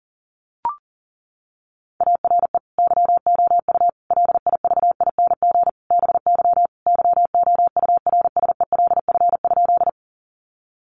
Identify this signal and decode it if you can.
Morse code, heard as a keyed tone, about 40 wpm, 700 Hz. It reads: ARE YOU LIVING BY YOURSELF?
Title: Sentences from Top 1100 Words 40wpm